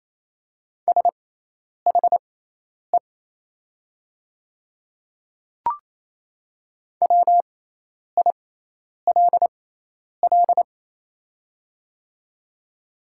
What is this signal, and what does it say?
Morse code, heard as a keyed tone, about 28 wpm, spaced as for 10 wpm, 700 Hz. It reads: SHE WILL
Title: 6x Character Spacing - Top 100 Words - 28wpm